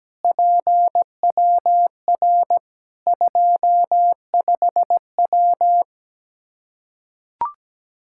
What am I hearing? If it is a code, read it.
Morse code, heard as a keyed tone, about 17 wpm, 700 Hz. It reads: PWR 25W
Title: Practice QSO - Exchange 2 - Info - 17wpm